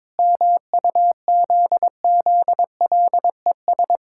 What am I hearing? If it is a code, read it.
Morse code, heard as a keyed tone, about 22 wpm, 700 Hz. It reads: MUZZLES
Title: Pangram Challenge v2 - 22wpm